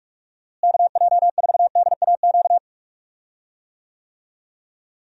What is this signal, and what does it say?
Morse code, heard as a keyed tone, about 41 wpm, 700 Hz. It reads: KJ4DAQ
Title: USA Call Signs - 6 Characters - Repeated 3x - 45wpm